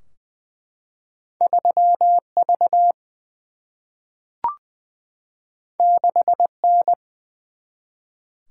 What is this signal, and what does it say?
Morse code, heard as a keyed tone, about 21 wpm, 700 Hz. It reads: 3 V 6 N
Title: Single-Letter Single-Number 20wpm